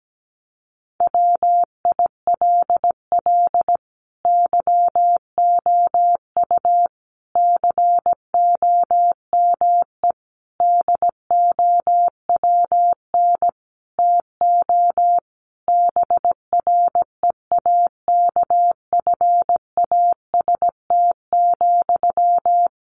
Morse code, heard as a keyed tone, about 17 wpm, 700 Hz. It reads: WILL YOU COME DOWN TO BREAKFAST,
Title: Sentences from Top 1500 Words 17wpm